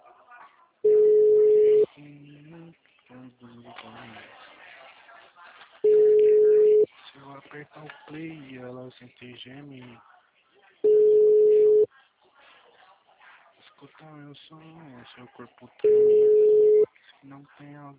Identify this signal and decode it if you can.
Ringback tone (Europe, 425 Hz), with speech.
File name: out- -2003